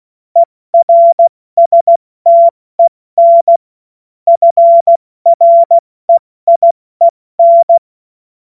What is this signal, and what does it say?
Morse code, heard as a keyed tone, about 16 wpm, 680 Hz. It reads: ERSTEN FREIEN
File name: Morsekurs / Witze / Witz 02. Dezember 2025 70 F80